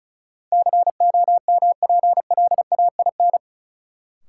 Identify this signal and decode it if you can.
Morse code, heard as a keyed tone, about 35 wpm, 700 Hz. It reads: COMPLAIN
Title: Sentences from Top 1300 Words - New Word Drill Alt - 35wpm